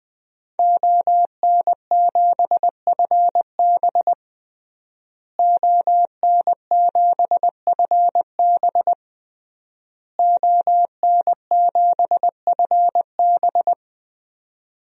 Morse code, heard as a keyed tone, about 21 wpm, 700 Hz. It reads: ON7FB ON7FB ON7FB
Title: DXCC-100 Call Signs - 5 Characters - Repeated 3x - 20wpm